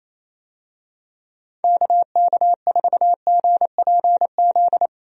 Morse code, heard as a keyed tone, about 27 wpm, 700 Hz. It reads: KK4GPZ
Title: USA Call Signs - 6 Characters - 28wpm